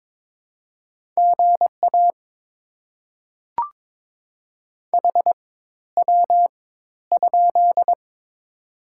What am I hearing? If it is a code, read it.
Morse code, heard as a keyed tone, about 22 wpm, spaced as for 12 wpm, 690 Hz. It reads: GA HW?
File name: Top 90 QSO Elements - 4x Character Spacing - 22wpm